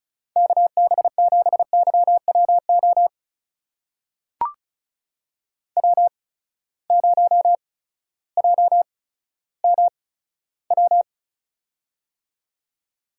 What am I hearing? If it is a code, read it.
Morse code, heard as a keyed tone, about 36 wpm, spaced as for 16 wpm, 700 Hz. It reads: KB7YWO W0JMW
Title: USA Call Signs - 8x Character Spacing - 35wpm